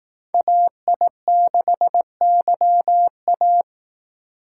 Morse code, heard as a keyed tone, about 19 wpm, 700 Hz. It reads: AI6YA
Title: Speed Racing - USA Call Signs 15wpm